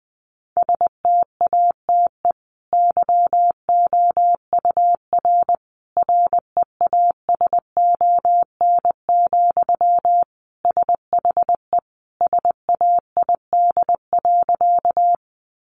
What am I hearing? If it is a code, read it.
Morse code, heard as a keyed tone, about 20 wpm, 700 Hz. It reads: STATE YOUR REASON, SHE SAID.
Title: Sentences from Top 700 Words 20wpm